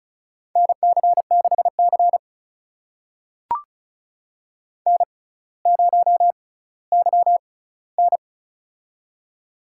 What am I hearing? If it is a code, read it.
Morse code, heard as a keyed tone, about 36 wpm, spaced as for 18 wpm, 700 Hz. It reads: NC6C N0YN
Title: USA Call Signs - 4 Characters - 6x Character Spacing - 35wpm